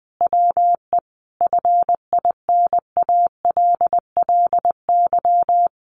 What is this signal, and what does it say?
Morse code, heard as a keyed tone, about 20 wpm, 700 Hz. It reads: WE FINALLY
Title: Sentences from Top 1500 Words 20wpm